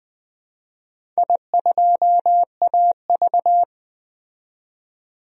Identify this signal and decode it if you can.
Morse code, heard as a keyed tone, about 20 wpm, 700 Hz. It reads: I2AV
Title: DXCC-100 Call Signs - 4 Characters - 20wpm